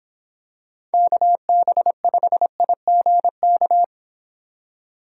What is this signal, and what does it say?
Morse code, heard as a keyed tone, about 25 wpm, 700 Hz. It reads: KB5IGK